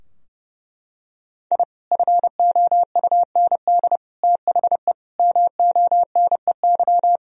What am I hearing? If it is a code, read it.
Morse code, heard as a keyed tone, about 32 wpm, 700 Hz. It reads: I FOUND THE MONEY
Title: Sentences from Top 1000 Words - Review Part 3 - 30wpm